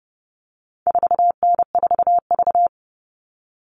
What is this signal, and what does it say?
Morse code, heard as a keyed tone, about 30 wpm, 700 Hz. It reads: 4N4V
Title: All Cut Numbers - 4-Digit 30wpm